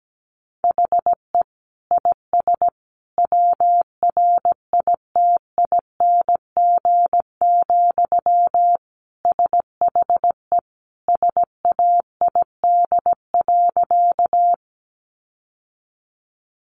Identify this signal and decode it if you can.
Morse code, heard as a keyed tone, about 17 wpm, 700 Hz. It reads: HE IS WRITING, SHE SAID.
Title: Sentences from Top 800 Words 17wpm